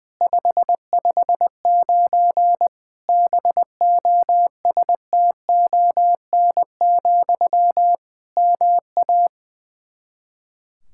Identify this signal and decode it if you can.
Morse code, heard as a keyed tone, about 20 wpm, 700 Hz. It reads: 559 BOSTON, MA
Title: Practice QSO - Exchange 1 - Beginning - 20wpm